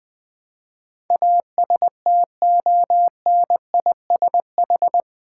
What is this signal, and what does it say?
Morse code, heard as a keyed tone, about 20 wpm, 700 Hz. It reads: ASTONISH
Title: Build A Word - 4 Letters At A Time - 11-Letter Words - 20wpm